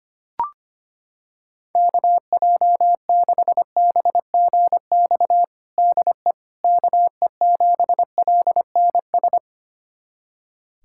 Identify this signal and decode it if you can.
Morse code, heard as a keyed tone, about 25 wpm, 700 Hz. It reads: KJ6BGX DE KE7LNS
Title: USA Callsign DE Callsign - 6 Characters - 25wpm